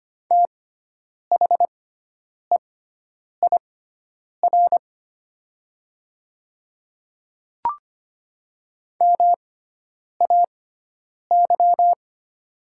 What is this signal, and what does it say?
Morse code, heard as a keyed tone, about 25 wpm, spaced as for 9 wpm, 700 Hz. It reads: THEIR MAY
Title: 6x Character Spacing - Top 100 Words - 25wpm